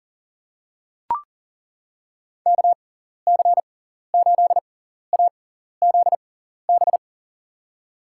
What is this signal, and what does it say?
Morse code, heard as a keyed tone, about 40 wpm, spaced as for 14 wpm, 700 Hz. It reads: KC8AZB
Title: USA Call Signs - 6x Character Spacing - 40wpm